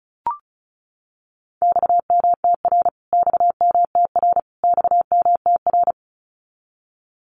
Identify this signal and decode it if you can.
Morse code, heard as a keyed tone, about 34 wpm, 700 Hz. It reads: XMTR XMTR XMTR
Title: Abbreviations Repeated 3x - 35wpm